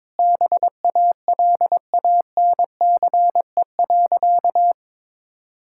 Morse code, heard as a keyed tone, about 22 wpm, 700 Hz. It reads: BALANCE.
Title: Sentences from Top 1000 Words - Review Part 6 - 22wpm